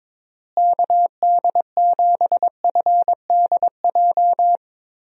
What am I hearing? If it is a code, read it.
Morse code, heard as a keyed tone, about 22 wpm, 700 Hz. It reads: KD7FDJ
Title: USA Call Signs - 6 Characters - Repeated 3x - 22wpm